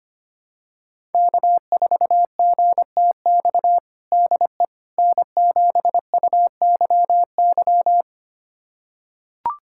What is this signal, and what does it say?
Morse code, heard as a keyed tone, about 25 wpm, 700 Hz. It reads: K4GTX DE N7UYY
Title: USA Callsign DE Callsign 25wpm